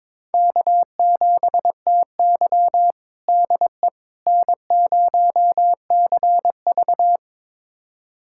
Morse code, heard as a keyed tone, about 22 wpm, 700 Hz. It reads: K7TY DE N0CV
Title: USA Callsign DE Callsign - 4 Characters - 22wpm